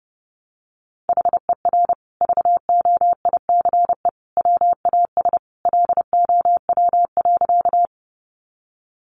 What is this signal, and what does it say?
Morse code, heard as a keyed tone, about 30 wpm, 700 Hz. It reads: HER VOICE WAS LOW.
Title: Sentences from Top 500 Words 30wpm